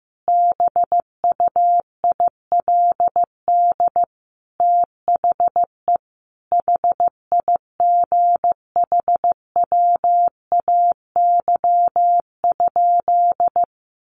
Morse code, heard as a keyed tone, about 15 wpm, 700 Hz. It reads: BUILD THE HIGHWAY?
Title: Sentences from Top 1400 Words 15wpm